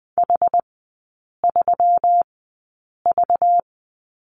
Morse code, heard as a keyed tone, about 20 wpm, spaced as for 8 wpm, 700 Hz. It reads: H3V
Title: 191 Lesson 31e - Letter v - Three Characters 20wpm